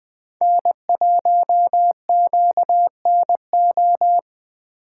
Morse code, heard as a keyed tone, about 20 wpm, 700 Hz. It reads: N1QNO
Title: USA Call Signs - 4x Character Spacing - 20wpm